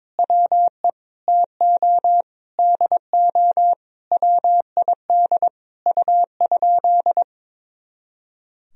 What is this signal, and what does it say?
Morse code, heard as a keyed tone, about 22 wpm, 700 Hz. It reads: WE TO DO WID U?
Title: Sentences from Top 100 Words and Abbreviations 22wpm